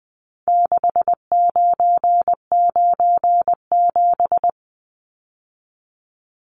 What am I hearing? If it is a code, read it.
Morse code, heard as a keyed tone, about 20 wpm, 700 Hz. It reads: 6997
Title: Numbers - 4-Digit 20wpm